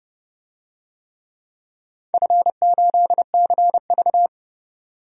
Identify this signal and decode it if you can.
Morse code, heard as a keyed tone, about 31 wpm, 700 Hz. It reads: F8CV
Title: DXCC-100 Call Signs - 4 Characters - 30wpm